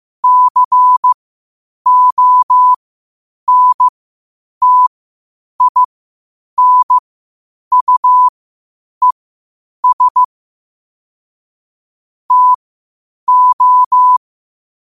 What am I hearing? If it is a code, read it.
Morse code, heard as a keyed tone, about 15 wpm, spaced as for 8 wpm, 1000 Hz. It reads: CONTINUES TO
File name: qotdfiles / 8wpm